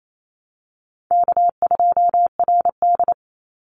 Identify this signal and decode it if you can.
Morse code, heard as a keyed tone, about 28 wpm, 700 Hz. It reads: K2RD
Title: Call Signs - 28wpm